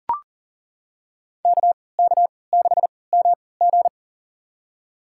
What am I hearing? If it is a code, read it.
Morse code, heard as a keyed tone, about 38 wpm, spaced as for 22 wpm, 700 Hz. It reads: KK6MG
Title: USA Call Signs - 5 Characters - 3x Character Spacing - 40wpm